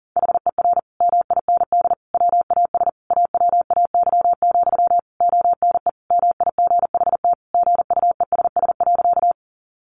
Morse code, heard as a keyed tone, about 40 wpm, 700 Hz. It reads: HER MIND WAS AWAY, ONE MIGHT GUESS.
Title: Sentences from Top 1500 Words 40wpm